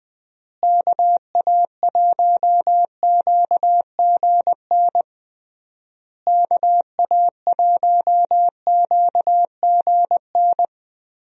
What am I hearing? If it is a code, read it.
Morse code, heard as a keyed tone, about 21 wpm, 700 Hz. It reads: KA1QGN KA1QGN
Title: USA Call Signs - 6 Characters - Repeated 3x - 20wpm